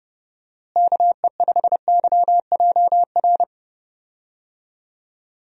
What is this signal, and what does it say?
Morse code, heard as a keyed tone, about 31 wpm, 700 Hz. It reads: KE5YJR
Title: USA Call Signs - 6 Characters - Repeated 3x - 30wpm